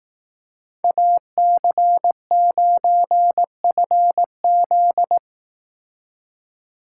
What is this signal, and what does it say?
Morse code, heard as a keyed tone, about 18 wpm, 700 Hz. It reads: AC9FZ
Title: Speed Racing - USA Call Signs 15wpm